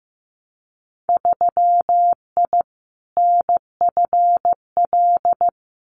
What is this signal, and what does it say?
Morse code, heard as a keyed tone, about 15 wpm, 700 Hz. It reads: 3I NFL
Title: Winter Field Day Exchanges 15wpm